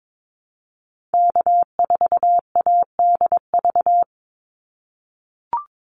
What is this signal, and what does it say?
Morse code, heard as a keyed tone, about 22 wpm, 700 Hz. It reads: K4ADV
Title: NA SOTA Activator Callsigns 22wpm